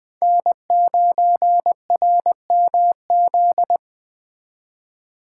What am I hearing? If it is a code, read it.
Morse code, heard as a keyed tone, about 21 wpm, 700 Hz. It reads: N9RMZ